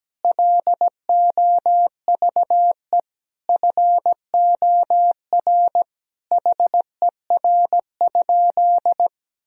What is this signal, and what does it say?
Morse code, heard as a keyed tone, about 17 wpm, 700 Hz. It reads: LOVE FOR HER?
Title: Sentences from Top 1000 Words - Review Part 1 - 17wpm